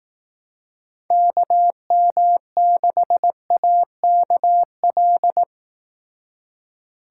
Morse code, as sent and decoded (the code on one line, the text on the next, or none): -.- -- -.... .- -.- .-..
KM6AKL